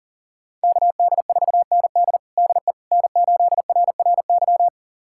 Morse code, heard as a keyed tone, about 40 wpm, 700 Hz. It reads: KD4ND DE N8RRY
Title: USA Callsign DE Callsign - 5 Characters - 40wpm